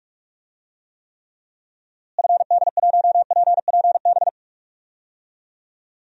Morse code, heard as a keyed tone, about 46 wpm, 700 Hz. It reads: FD1PPB
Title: DXCC-100 Call Signs - 6 Characters - Repeated 3x - 45wpm